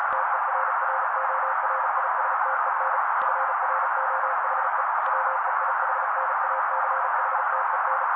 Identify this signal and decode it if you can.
Morse code, heard as a keyed tone, about 37 wpm, 530 Hz. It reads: TERRORISTEN FROHE WEIHNACHTEN
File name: daily / 35wpm noise